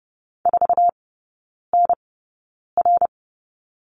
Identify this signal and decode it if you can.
Morse code, heard as a keyed tone, about 30 wpm, spaced as for 9 wpm, 700 Hz. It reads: 4NR
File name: Lesson 17e - Letter u - Three Characters 30wpm